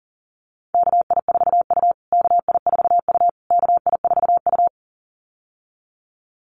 Morse code, heard as a keyed tone, about 40 wpm, 700 Hz. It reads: KI4U KI4U KI4U
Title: USA Call Signs Repeated 3x 40wpm